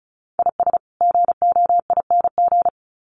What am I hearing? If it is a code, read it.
Morse code, heard as a keyed tone, about 34 wpm, 700 Hz. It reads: IS GOING
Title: Sets of 2 Words 35wpm